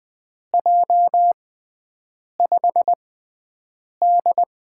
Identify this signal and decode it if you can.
Morse code, heard as a keyed tone, about 20 wpm, spaced as for 7 wpm, 700 Hz. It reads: J5D
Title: USA Call Signs - 4 Characters - 6x Character Spacing - 20wpm